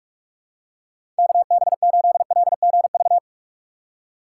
Morse code, heard as a keyed tone, about 44 wpm, 700 Hz. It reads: KB8LGV
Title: USA Call Signs - 3x Character Spacing - 45wpm